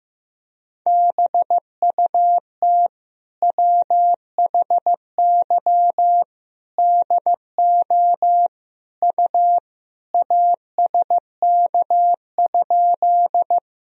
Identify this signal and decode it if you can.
Morse code, heard as a keyed tone, about 15 wpm, 700 Hz. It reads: BUT WHY DO U ASK?
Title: Sentences from Top 200 Words and Abbreviations 15wpm